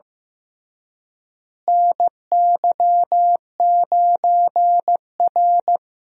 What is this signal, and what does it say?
Morse code, heard as a keyed tone, about 15 wpm, 700 Hz. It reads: NY9R
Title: USA Call Signs - 4 Characters - Repeated 3x - 15wpm